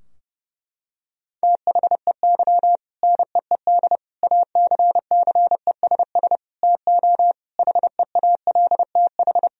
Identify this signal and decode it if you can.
Morse code, heard as a keyed tone, about 30 wpm, 700 Hz. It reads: THEY NEED ACCESS TO HEALTH
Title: Sentences from Top 1000 Words - Review Part 4 - 30wpm